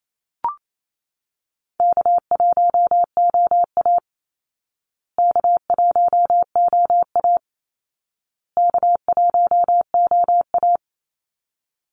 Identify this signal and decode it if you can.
Morse code, heard as a keyed tone, about 28 wpm, 700 Hz. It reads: K1OA K1OA K1OA
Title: Easy Call Signs Repeated 3x - 28wpm